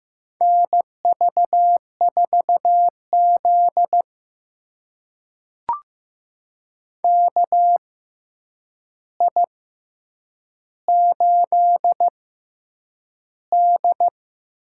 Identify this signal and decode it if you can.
Morse code, heard as a keyed tone, about 15 wpm, spaced as for 8 wpm, 700 Hz. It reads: NV4Z KI8D
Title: USA Call Signs - 6x Character Spacing - 15wpm